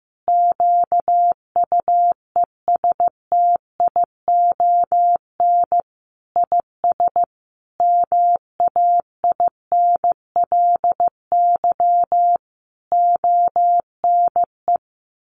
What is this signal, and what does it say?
Morse code, heard as a keyed tone, about 15 wpm, 700 Hz. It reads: QUESTION IS MAINLY ONE
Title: Sentences from Top 1400 Words 15wpm